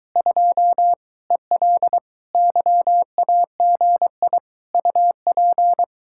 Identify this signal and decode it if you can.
Morse code, heard as a keyed tone, about 22 wpm, 700 Hz. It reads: 2 EL YAGI UP